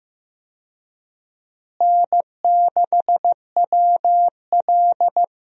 Morse code, heard as a keyed tone, about 15 wpm, 700 Hz. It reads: N6WL
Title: USA Call Signs - 4 Characters - Repeated 3x - 15wpm